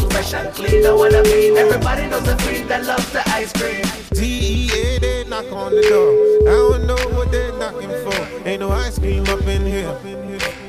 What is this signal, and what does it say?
Ringback tone (Europe, 425 Hz), with noise.